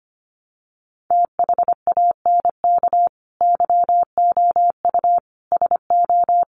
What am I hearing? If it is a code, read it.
Morse code, heard as a keyed tone, about 25 wpm, 700 Hz. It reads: THANK YOU SO
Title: Sets of 4 Words 25wpm